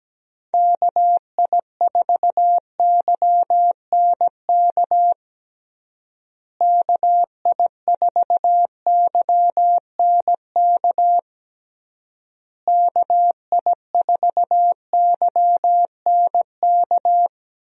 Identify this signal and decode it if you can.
Morse code, heard as a keyed tone, about 17 wpm, 700 Hz. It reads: KI4YNK KI4YNK KI4YNK
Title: USA Call Signs - 6 Characters - Repeated 3x - 17wpm